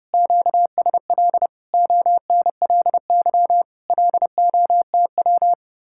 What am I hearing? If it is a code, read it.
Morse code, heard as a keyed tone, about 29 wpm, 700 Hz. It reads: QSL ONLY LOTW